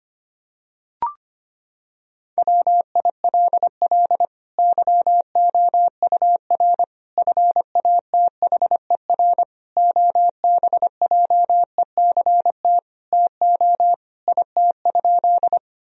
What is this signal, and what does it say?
Morse code, heard as a keyed tone, about 25 wpm, 700 Hz. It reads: WILL YOUR FATHER OBJECT TO IT?
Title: Sentences from Top 1000 Words - Review Part 3 - 25wpm